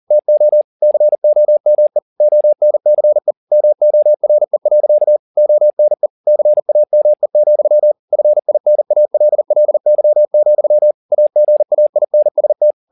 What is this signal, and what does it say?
Morse code, heard as a keyed tone, about 40 wpm, 600 Hz. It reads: TO COME ONCE MORE. ONE CAME, FINALLY, AGAINST
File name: the old man and the sea 2 40wpm